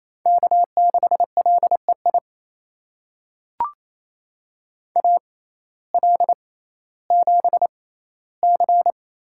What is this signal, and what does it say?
Morse code, heard as a keyed tone, about 28 wpm, spaced as for 15 wpm, 700 Hz. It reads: K6LEI AL7C
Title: USA Call Signs - 5 Characters - 6x Character Spacing - 28wpm